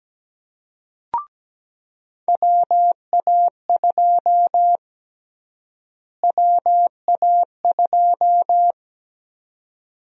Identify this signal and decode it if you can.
Morse code, heard as a keyed tone, about 17 wpm, 700 Hz. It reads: WA2 WA2
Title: USA Call Sign Prefixes - Repeated 2x - 17wpm